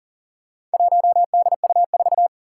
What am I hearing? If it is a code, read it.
Morse code, heard as a keyed tone, about 39 wpm, 700 Hz. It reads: 1DU4
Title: Four Challenging Letter-Number Combinations 40wpm